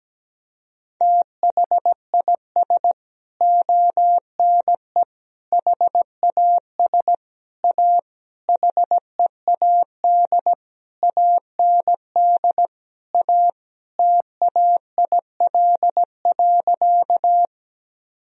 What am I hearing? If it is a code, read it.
Morse code, heard as a keyed tone, about 17 wpm, 700 Hz. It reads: THIS ONE HAS A HEAD AND A TAIL.
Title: Sentences from Top 1000 Words - Review Part 3 - 17wpm